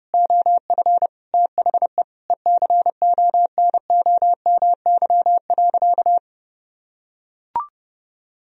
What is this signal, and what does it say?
Morse code, heard as a keyed tone, about 30 wpm, 700 Hz. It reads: OF THE ECONOMY.
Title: Sentences from Top 1000 Words - Review Part 3 - 30wpm